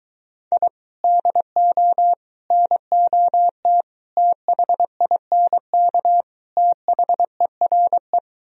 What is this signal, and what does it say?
Morse code, heard as a keyed tone, about 23 wpm, 700 Hz. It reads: I DO NOT THINK THERE